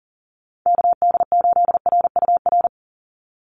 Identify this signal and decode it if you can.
Morse code, heard as a keyed tone, about 40 wpm, 700 Hz. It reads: KD8RUR
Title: USA Call Signs 40wpm